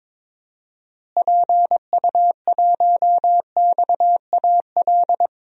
Morse code, heard as a keyed tone, about 21 wpm, 700 Hz. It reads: PU1XAL
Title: DXCC-100 Call Signs - 6 Characters - Repeated 3x - 22wpm